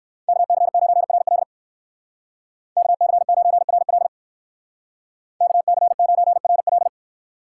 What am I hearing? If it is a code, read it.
Morse code, heard as a keyed tone, about 55 wpm, 700 Hz. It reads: KC9RL KC9RL KC9RL
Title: Speed Racing - USA 2x2 Call Signs 45wpm